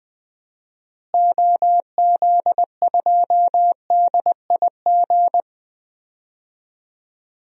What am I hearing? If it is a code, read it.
Morse code, heard as a keyed tone, about 20 wpm, 700 Hz. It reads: OZ2DIG
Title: DXCC-100 Call Signs - 6 Characters - Repeated 2x - 20wpm